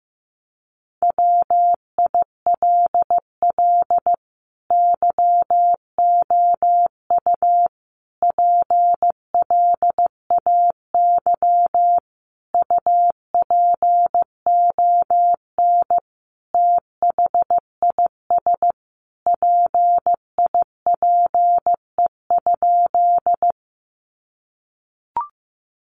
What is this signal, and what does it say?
Morse code, heard as a keyed tone, about 15 wpm, 700 Hz. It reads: WILL YOU PLAY UPON THIS PIPE?
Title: Sentences from Top 1500 Words 15wpm